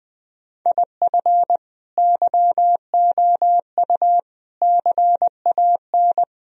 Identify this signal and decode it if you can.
Morse code, heard as a keyed tone, about 20 wpm, 700 Hz. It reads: IF YOU CAN
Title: Sentences from Top 1000 Words - Review Part 2 - 20wpm